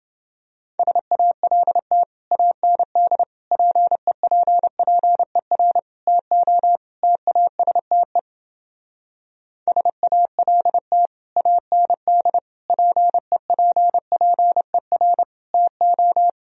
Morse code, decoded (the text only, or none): SALT AND PEPPER TO TASTE SALT AND PEPPER TO